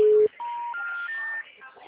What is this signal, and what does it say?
Special information tone, with speech.
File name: out- -2003- - - .29619